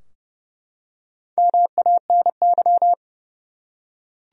Morse code, heard as a keyed tone, about 30 wpm, 700 Hz. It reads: MANY
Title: Top 100 Words 30wpm